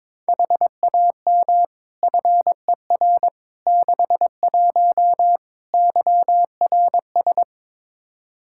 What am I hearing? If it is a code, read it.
Morse code, heard as a keyed tone, about 21 wpm, 700 Hz. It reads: HAM FER 61 YRS
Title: Practice QSO - Exchange 3 - More Info - 22wpm